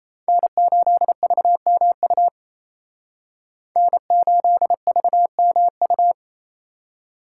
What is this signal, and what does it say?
Morse code, heard as a keyed tone, about 36 wpm, 700 Hz. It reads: N8VMU N8VMU